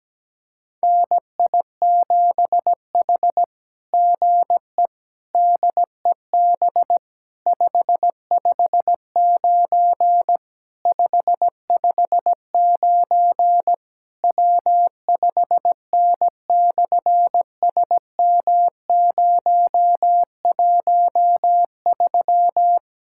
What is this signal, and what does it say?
Morse code, heard as a keyed tone, about 17 wpm, 700 Hz. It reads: NI7H GE DEB 559 559 W5N/SM013
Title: SOTA QSO Practice 17wpm